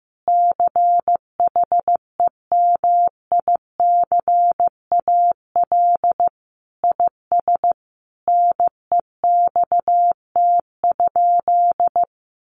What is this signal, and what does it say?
Morse code, heard as a keyed tone, about 15 wpm, 700 Hz. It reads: CHEMICAL IS NEXT?
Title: Sentences from Top 1200 Words 15wpm